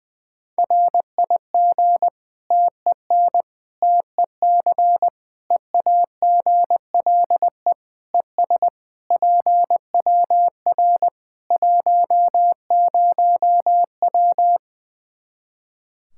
Morse code, heard as a keyed tone, about 21 wpm, 700 Hz. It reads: RIG TEN TEC EAGLE ES PWR 10W
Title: K1USN SST QSO Practice 20wpm